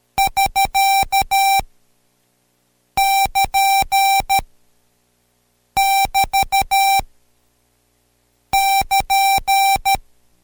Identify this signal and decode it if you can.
Morse code, heard as a keyed tone, about 13 wpm, 780 Hz. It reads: <SK> <KN> <BT> <KN>
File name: Code Quick / AudioDownld2 / Seg127-Code Quick Prosigns 2